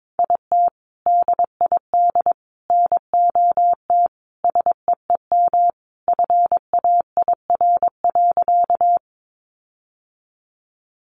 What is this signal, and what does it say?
Morse code, heard as a keyed tone, about 22 wpm, 700 Hz. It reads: IT DID NOT SEEM FAIR.
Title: Sentences from Top 1300 Words 22wpm